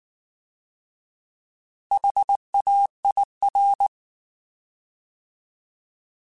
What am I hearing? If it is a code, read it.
Morse code, heard as a keyed tone, about 19 wpm, 800 Hz. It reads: HAIR